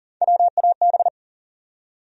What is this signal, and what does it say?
Morse code, heard as a keyed tone, about 40 wpm, 700 Hz. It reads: WAB
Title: USA Call Sign Postfixes - 3 Letters 40wpm